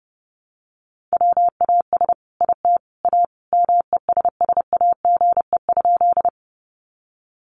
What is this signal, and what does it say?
Morse code, heard as a keyed tone, about 30 wpm, 700 Hz. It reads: WAS IT A MESSAGE?
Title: Sentences from Top 1100 Words 30wpm